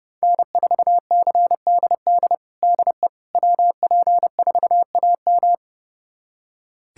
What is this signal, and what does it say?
Morse code, heard as a keyed tone, about 32 wpm, 700 Hz. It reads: N4CDD DE WP4AM
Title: USA Callsign DE Callsign - 5 Characters - 30wpm